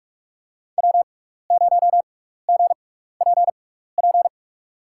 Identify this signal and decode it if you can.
Morse code, heard as a keyed tone, about 43 wpm, spaced as for 15 wpm, 700 Hz. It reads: W0GPP
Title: USA Call Signs - 5 Characters - 6x Character Spacing - 45wpm